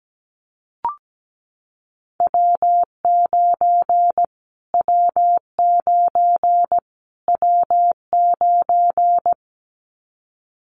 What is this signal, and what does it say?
Morse code, heard as a keyed tone, about 17 wpm, 700 Hz. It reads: W9 W9 W9
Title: Callsign Prefixes for Top 25 DXCC - Repeated 3x - 17wpm